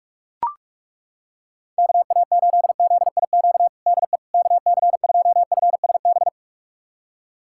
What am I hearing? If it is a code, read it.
Morse code, heard as a keyed tone, about 41 wpm, 700 Hz. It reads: KA8ZIQ DE KC2FSB
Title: USA Callsign DE Callsign 45wpm